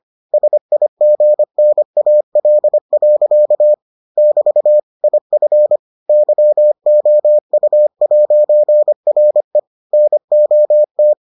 Morse code, heard as a keyed tone, about 25 wpm, 600 Hz. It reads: SIGNAL. <BT> IF YOU'RE NOT